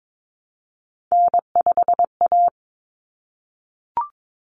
Morse code, heard as a keyed tone, about 22 wpm, 700 Hz. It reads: N5A
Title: Common Cut Numbers - 3-Digit 22wpm